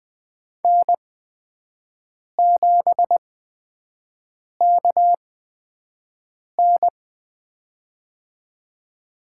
Morse code, heard as a keyed tone, about 20 wpm, spaced as for 5 wpm, 700 Hz. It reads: N7KN
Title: USA Call Signs - 8x Character Spacing - 20wpm